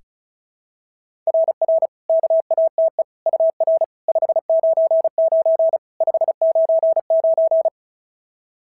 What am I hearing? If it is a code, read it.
Morse code, heard as a keyed tone, about 36 wpm, 660 Hz. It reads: RR KATE UR 599 599
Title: SOTA QSO Practice 35wpm